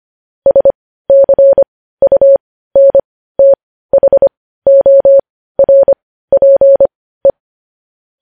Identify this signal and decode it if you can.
Morse code, heard as a keyed tone, about 25 wpm, spaced as for 15 wpm, 550 Hz. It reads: SCUNTHORPE
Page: news headlines in morse code at 15 wpm